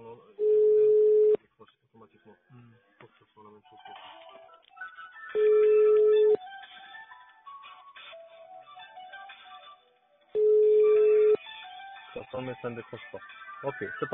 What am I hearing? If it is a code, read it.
Ringback tone (Europe, 425 Hz), with speech.